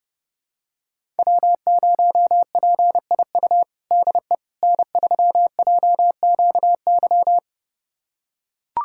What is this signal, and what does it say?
Morse code, heard as a keyed tone, about 30 wpm, 700 Hz. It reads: W0PIU DE N3JQY
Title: USA Callsign DE Callsign - 5 Characters - 30wpm